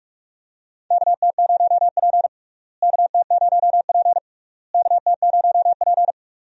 Morse code, heard as a keyed tone, about 47 wpm, 700 Hz. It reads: KT0P KT0P KT0P
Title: USA Call Signs - 4 Characters - Repeated 3x - 45wpm